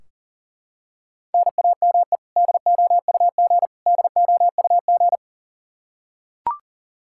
Morse code, heard as a keyed tone, about 40 wpm, 700 Hz. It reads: NAME DOUG DOUG
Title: Practice QSO - Exchange 1 - Beginning - 40wpm